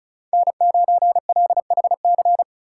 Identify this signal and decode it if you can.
Morse code, heard as a keyed tone, about 34 wpm, 700 Hz. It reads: N9LHC
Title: USA Call Signs - 4x Character Spacing - 35wpm